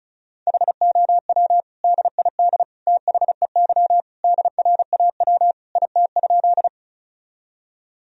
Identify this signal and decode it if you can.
Morse code, heard as a keyed tone, about 36 wpm, 700 Hz. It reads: HOW DID THEY DRAW IT?
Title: Sentences from Top 1000 Words - Review Part 1 - 35wpm